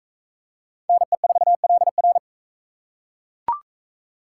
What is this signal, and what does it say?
Morse code, heard as a keyed tone, about 42 wpm, 700 Hz. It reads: NE4LR